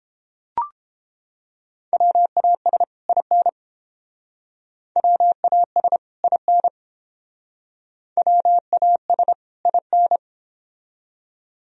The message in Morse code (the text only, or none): WAS IN WAS IN WAS IN